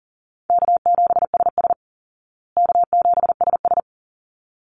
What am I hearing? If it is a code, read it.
Morse code, heard as a keyed tone, about 40 wpm, 700 Hz. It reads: K7SS K7SS
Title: Easy Call Signs Repeated 2x - 40wpm